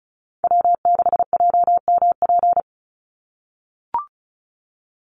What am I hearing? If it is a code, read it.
Morse code, heard as a keyed tone, about 34 wpm, 700 Hz. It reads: W6JMP
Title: NA SOTA Chaser Callsigns 35wpm